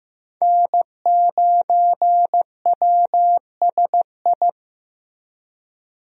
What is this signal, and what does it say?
Morse code, heard as a keyed tone, about 15 wpm, 700 Hz. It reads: N9WSI
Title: USA Call Signs - 5 Characters - Repeated 3x - 15wpm